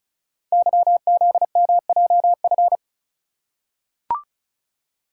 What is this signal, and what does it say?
Morse code, heard as a keyed tone, about 35 wpm, 690 Hz. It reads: YZMJF
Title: Code Groups - 5 Letters - 35wpm